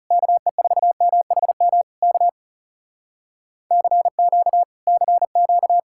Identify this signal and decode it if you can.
Morse code, heard as a keyed tone, about 36 wpm, 700 Hz. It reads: KE4MHM K CQ CQ
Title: Speed-Racing Common QSO Phrases - 30wpm